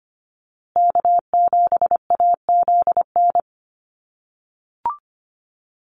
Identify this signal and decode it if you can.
Morse code, heard as a keyed tone, about 25 wpm, 700 Hz. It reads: K7AZN
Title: USA Call Signs Repeated 2x 25wpm